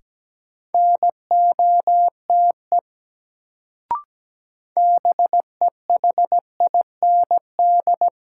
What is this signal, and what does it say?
Morse code, heard as a keyed tone, about 17 wpm, 700 Hz. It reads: NOTE BEHIND
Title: Sentences from Top 500 Words - New Word Drill - 17wpm